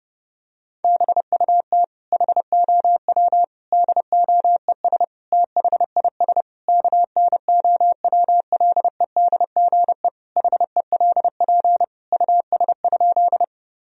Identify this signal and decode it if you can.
Morse code, heard as a keyed tone, about 32 wpm, 700 Hz. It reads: BUT HOW DOES THIS KNOWLEDGE HELP US?
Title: Sentences from Top 1000 Words - Review Part 3 - 30wpm